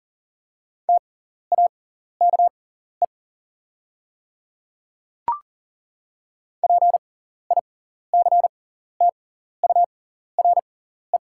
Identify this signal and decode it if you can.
Morse code, heard as a keyed tone, about 39 wpm, spaced as for 14 wpm, 700 Hz. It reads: TAKE PICTURE
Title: 6x Character Spacing - Top 300 Words - 40wpm